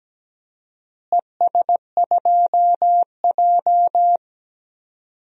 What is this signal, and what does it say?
Morse code, heard as a keyed tone, about 17 wpm, 700 Hz. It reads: ES2J
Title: DXCC-100 Call Signs - 4 Characters - Repeated 2x - 17wpm